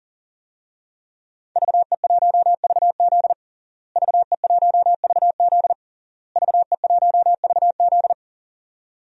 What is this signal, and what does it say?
Morse code, heard as a keyed tone, about 40 wpm, 700 Hz. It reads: VE1VZ VE1VZ VE1VZ
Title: DXCC-100 Call Signs - 5 Characters - Repeated 3x - 40wpm